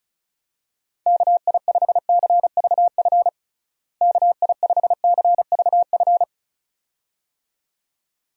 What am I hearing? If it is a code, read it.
Morse code, heard as a keyed tone, about 36 wpm, 700 Hz. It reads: KI5CVF KI5CVF
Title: USA Call Signs - 6 Characters - Repeated 2x - 35wpm